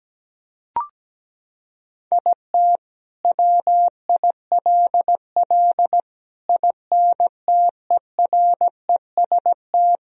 Morse code, heard as a keyed tone, about 17 wpm, 700 Hz. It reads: IT WILL INTEREST
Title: Sentences from Top 1000 Words - Review Part 2 - 17wpm